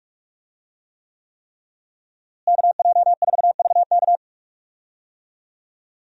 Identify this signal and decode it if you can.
Morse code, heard as a keyed tone, about 41 wpm, 700 Hz. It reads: KJ4VK
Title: USA Call Signs - 5 Characters - Repeated 3x - 45wpm